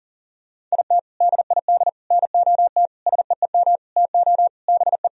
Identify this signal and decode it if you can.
Morse code, heard as a keyed tone, about 40 wpm, 700 Hz. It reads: IT DID NOT SEEM TO BE
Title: Sentences from Top 1000 Words - Review Part 2 - 40wpm